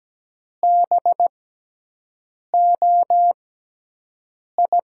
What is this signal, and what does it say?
Morse code, heard as a keyed tone, about 17 wpm, spaced as for 6 wpm, 700 Hz. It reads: BOI
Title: USA Call Signs - 6x Character Spacing - 17wpm